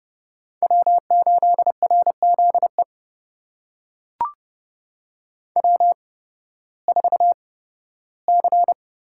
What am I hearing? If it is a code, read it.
Morse code, heard as a keyed tone, about 31 wpm, spaced as for 16 wpm, 700 Hz. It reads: W8RZE W4C
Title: USA Call Signs - 5 Characters - 8x Character Spacing - 30wpm